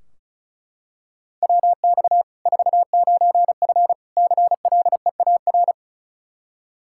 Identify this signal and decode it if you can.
Morse code, heard as a keyed tone, about 34 wpm, 700 Hz. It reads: WX 49F CLEAR
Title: Practice QSO - Exchange 2 - Info - 35wpm